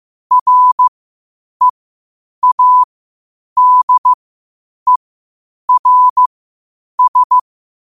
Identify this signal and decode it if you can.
Morse code, heard as a keyed tone, about 15 wpm, spaced as for 8 wpm, 1000 Hz. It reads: READERS
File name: qotdfiles / 8wpm